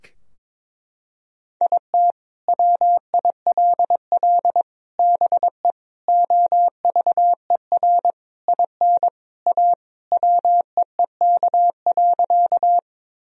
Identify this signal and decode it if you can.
Morse code, heard as a keyed tone, about 22 wpm, 700 Hz. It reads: IT WILL BE OVER IN A WEEK.
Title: Sentences from Top 1000 Words - Review Part 2 - 22wpm